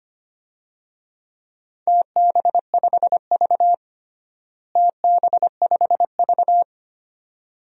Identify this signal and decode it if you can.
Morse code, heard as a keyed tone, about 25 wpm, 700 Hz. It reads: TB5V TB5V
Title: DXCC-100 Call Signs - 4 Characters - Repeated 2x - 25wpm